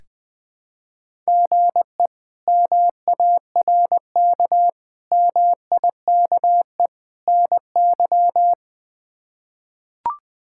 Morse code, heard as a keyed tone, about 20 wpm, 700 Hz. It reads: GE MARK MIKE NY
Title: K1USN SST QSO Practice - 20wpm